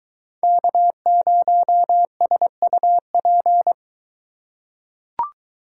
Morse code, heard as a keyed tone, about 22 wpm, 700 Hz. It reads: K0SUP